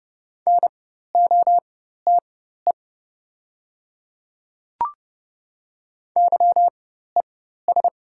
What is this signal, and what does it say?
Morse code, heard as a keyed tone, about 30 wpm, spaced as for 14 wpm, 690 Hz. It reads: NOTE YES
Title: 4x Character Spacing - Top 500 Words - 30wpm